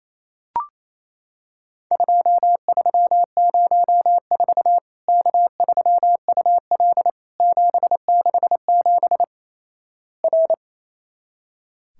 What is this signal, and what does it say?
Morse code, heard as a keyed tone, about 28 wpm, 700 Hz. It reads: 2304 K3UL 767 R
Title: Worked All Europe DX Contest - QTC Exchange - 28wpm